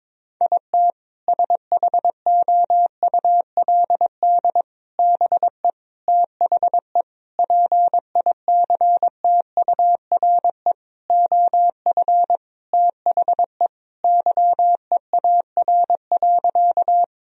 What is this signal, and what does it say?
Morse code, heard as a keyed tone, about 22 wpm, 700 Hz. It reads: IT SHOULD BE THE PICTURE OF THE YEAR.
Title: Sentences from Top 1000 Words - Review Part 1 - 22wpm